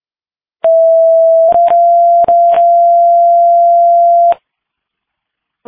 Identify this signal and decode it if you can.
Two-tone page, heard as a one-second tone, then a three-second tone, about 660 then 690 Hz. It is heard from 0.6 to 4.3 s.
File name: Bremer All Call